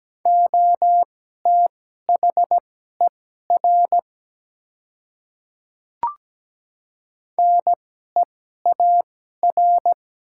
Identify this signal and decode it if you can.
Morse code, heard as a keyed tone, about 17 wpm, spaced as for 12 wpm, 690 Hz. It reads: OTHER NEAR
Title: 2x Character Spacing - Top 300 Words - 17wpm